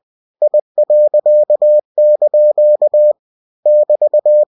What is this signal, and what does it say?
Morse code, heard as a keyed tone, about 20 wpm, 600 Hz. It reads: I.) <BT>
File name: cw-books / winnie-the-pooh / winnie-20wpm-page-0002